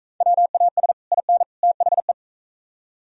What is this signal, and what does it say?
Morse code, heard as a keyed tone, about 42 wpm, 700 Hz. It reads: WAS IN THE